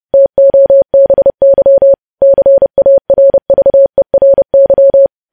Morse code, heard as a keyed tone, about 30 wpm, 550 Hz. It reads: TOBY CARVERY
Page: news headlines in morse code at 30 wpm